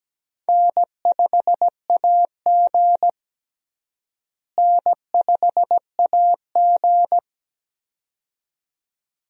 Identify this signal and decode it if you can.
Morse code, heard as a keyed tone, about 17 wpm, 700 Hz. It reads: N5AG N5AG
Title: USA Call Signs - 4 Characters - Repeated 2x - 17wpm